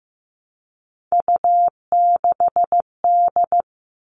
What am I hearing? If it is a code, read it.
Morse code, heard as a keyed tone, about 15 wpm, 700 Hz. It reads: U6D
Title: All Cut Numbers - 3-Digit 15wpm